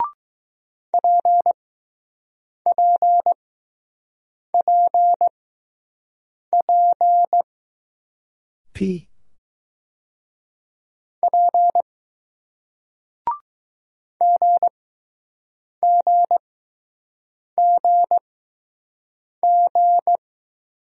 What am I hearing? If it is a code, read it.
Morse code, heard as a keyed tone, about 20 wpm, spaced as for 6 wpm, 700 Hz. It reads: PPPP P GGGG